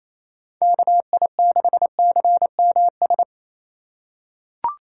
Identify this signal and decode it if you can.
Morse code, heard as a keyed tone, about 28 wpm, 700 Hz. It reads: KI6CMS
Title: USA Call Signs - 2x Character Spacing - 28wpm